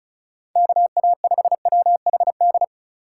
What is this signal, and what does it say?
Morse code, heard as a keyed tone, about 36 wpm, 700 Hz. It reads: KA5WHD
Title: USA Call Signs - 6 Characters - 4x Character Spacing - 35wpm